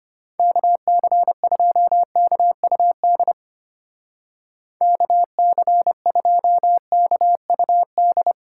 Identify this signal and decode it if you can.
Morse code, heard as a keyed tone, about 30 wpm, 700 Hz. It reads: KC2KUD KC2KUD
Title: Speed Racing - USA 2x3 Call Signs 25wpm